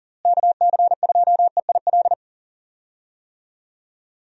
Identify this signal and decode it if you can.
Morse code, heard as a keyed tone, about 40 wpm, 700 Hz. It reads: KC2EIL
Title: USA Call Signs - 6 Characters - Repeated 2x - 40wpm